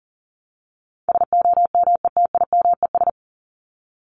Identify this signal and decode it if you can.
Morse code, heard as a keyed tone, about 39 wpm, 700 Hz. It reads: SOMETIMES
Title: ES:S Suffix - Top 100 Words with Base Words 40wpm